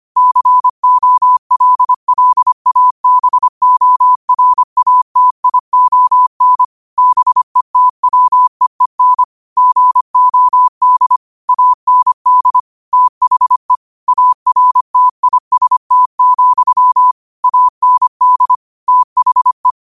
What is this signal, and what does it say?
Morse code, heard as a keyed tone, about 25 wpm, 1000 Hz. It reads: COLLABORATION BETWEEN GOD AND THE ARTIST, AND THE